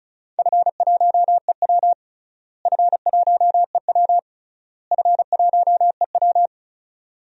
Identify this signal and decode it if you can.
Morse code, heard as a keyed tone, about 34 wpm, 700 Hz. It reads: F1EW F1EW F1EW
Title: DXCC-100 Call Signs - 4 Characters - Repeated 3x - 35wpm